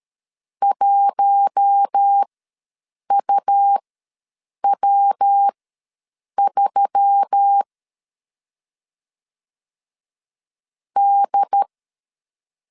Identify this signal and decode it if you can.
Morse code, heard as a keyed tone, about 13 wpm, spaced as for 5 wpm, 800 Hz. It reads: 1UW3D
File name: LEZIONI / 5 LBVC / 5 LBVC ws 12wpm 3159